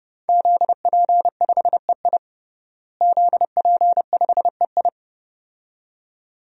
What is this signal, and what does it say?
Morse code, heard as a keyed tone, about 31 wpm, 700 Hz. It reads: ZP5EI ZP5EI
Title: DXCC-100 Call Signs - 5 Characters - Repeated 2x - 30wpm